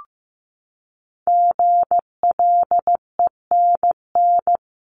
Morse code, heard as a keyed tone, about 15 wpm, 700 Hz. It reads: GLENN
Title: CWT Exchange 15wpm